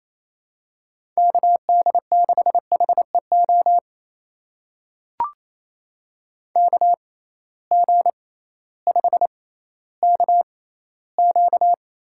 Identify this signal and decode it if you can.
Morse code, heard as a keyed tone, about 28 wpm, spaced as for 15 wpm, 700 Hz. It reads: KD6HEO KG5KQ
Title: USA Call Signs - 6 Characters - 6x Character Spacing - 28wpm